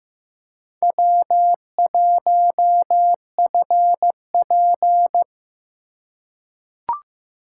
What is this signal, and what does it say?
Morse code, heard as a keyed tone, about 15 wpm, 700 Hz. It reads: W1FP
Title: USA Call Signs - 4 Characters - 3x Character Spacing - 15wpm